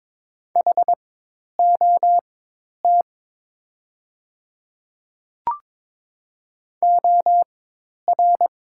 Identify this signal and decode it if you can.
Morse code, heard as a keyed tone, about 22 wpm, spaced as for 10 wpm, 700 Hz. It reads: HOT OR
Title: 4x Character Spacing - Top 100 Words - 22wpm